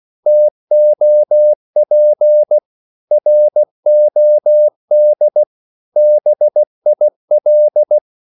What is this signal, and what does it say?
Morse code, heard as a keyed tone, about 16 wpm, 600 Hz. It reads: TOP ROD BIL
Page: Korte danske ord 16wpm